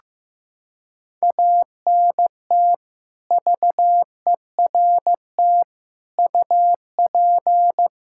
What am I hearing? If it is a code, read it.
Morse code, heard as a keyed tone, about 15 wpm, 700 Hz. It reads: ANT VERT UP
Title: Practice QSO - Exchange 2 - Info - 15wpm